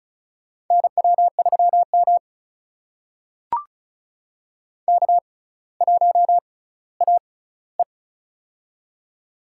Morse code, heard as a keyed tone, about 34 wpm, spaced as for 18 wpm, 700 Hz. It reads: NW3M K1AE
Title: USA Call Signs - 4 Characters - 6x Character Spacing - 35wpm